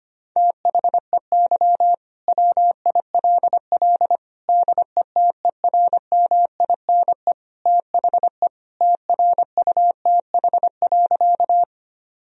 Morse code, heard as a keyed tone, about 25 wpm, 700 Hz. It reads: THEY WILL DETERMINE THE TRUTH.
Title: Sentences from Top 1000 Words - Review Part 4 - 25wpm